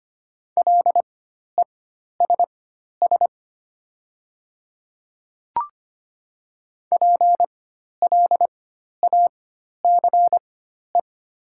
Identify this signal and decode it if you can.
Morse code, heard as a keyed tone, about 25 wpm, spaced as for 12 wpm, 700 Hz. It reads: LESS PLACE
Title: 4x Character Spacing - Top 500 Words - 25wpm